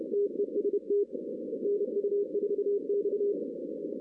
Morse code, heard as a keyed tone, about 30 wpm, 410 Hz. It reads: TEST NU4X
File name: 2016 arrl dx cw / 2016-ARRL-DX-CW-K5TR- - 0-9